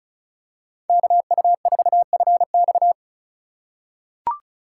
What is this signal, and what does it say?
Morse code, heard as a keyed tone, about 36 wpm, 700 Hz. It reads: KU4FX
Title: USA Call Signs - 5 Characters - 4x Character Spacing - 35wpm